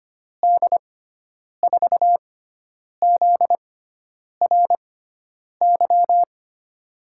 Morse code, heard as a keyed tone, about 25 wpm, spaced as for 9 wpm, 700 Hz. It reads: D4ZRY
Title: USA Call Signs - 6 Characters - 6x Character Spacing - 25wpm